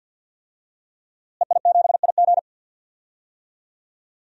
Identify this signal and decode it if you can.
Morse code, heard as a keyed tone, about 46 wpm, 700 Hz. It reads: EI7IG
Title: DXCC-100 Call Signs - 5 Characters - Repeated 2x - 50wpm